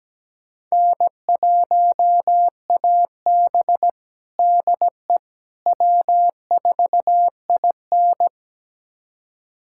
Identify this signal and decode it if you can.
Morse code, heard as a keyed tone, about 17 wpm, 700 Hz. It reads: N1AB DE W4IN
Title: USA Callsign DE Callsign - 4 Characters - 17wpm